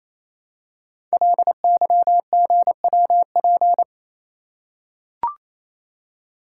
Morse code, heard as a keyed tone, about 29 wpm, 700 Hz. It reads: LYGWP
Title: Code Groups - 5 Letter:Numbers - 28wpm